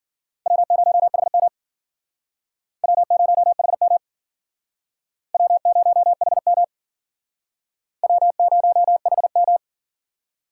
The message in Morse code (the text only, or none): W0HM W0HM W0HM W0HM